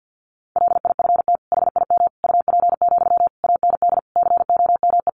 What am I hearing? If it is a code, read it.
Morse code, heard as a keyed tone, about 50 wpm, 700 Hz. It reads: LIFT HIM UP, AND COME
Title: Sentences from Top 1000 Words 50wpm